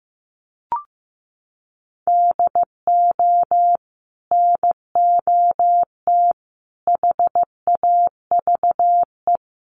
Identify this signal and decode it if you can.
Morse code, heard as a keyed tone, about 15 wpm, 700 Hz. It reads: DO NOT HAVE
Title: Sets of 4 Words 15wpm